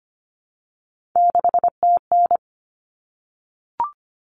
Morse code, heard as a keyed tone, about 25 wpm, 700 Hz. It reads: 6TN
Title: Common Cut Numbers - 3-Digit 25wpm